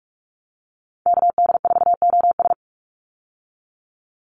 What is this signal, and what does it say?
Morse code, heard as a keyed tone, about 44 wpm, 700 Hz. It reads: KD4OS
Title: Call Signs Repeated 3x - 45wpm